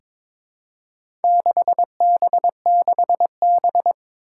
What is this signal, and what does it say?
Morse code, heard as a keyed tone, about 23 wpm, 700 Hz. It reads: 6B6B
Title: Four Challenging Letter-Number Combinations 22wpm